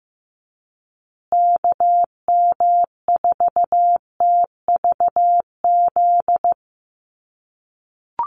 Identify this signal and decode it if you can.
Morse code, heard as a keyed tone, about 15 wpm, 700 Hz. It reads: KM4TVZ
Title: USA Call Signs Repeated 3x 15wpm